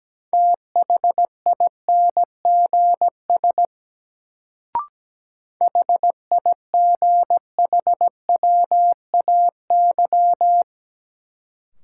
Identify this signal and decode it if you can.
Morse code, heard as a keyed tone, about 17 wpm, 700 Hz. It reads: THINGS HIGHWAY
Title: Sentences from Top 1400 Words - New Word Drill Alt - 17wpm